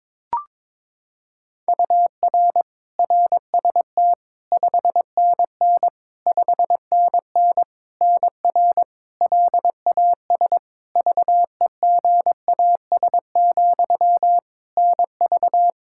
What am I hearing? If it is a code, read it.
Morse code, heard as a keyed tone, about 22 wpm, 700 Hz. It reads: UR RST 5NN 5NN NR LAS VEGAS, NV
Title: K1USN SST QSO Practice 22wpm